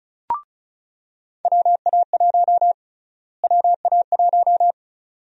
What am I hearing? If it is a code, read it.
Morse code, heard as a keyed tone, about 36 wpm, 700 Hz. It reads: WA1 WA1
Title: USA Call Sign Prefixes - Repeated 2x - 35wpm